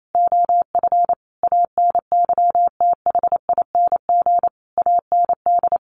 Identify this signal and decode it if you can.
Morse code, heard as a keyed tone, about 28 wpm, 700 Hz. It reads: OF ANYTHING AND
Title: Sentences from Top 1200 Words 28wpm